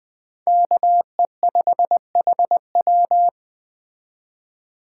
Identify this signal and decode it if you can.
Morse code, heard as a keyed tone, about 21 wpm, 700 Hz. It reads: KE5HW
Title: Speed Racing - USA 2x2 Call Signs 17wpm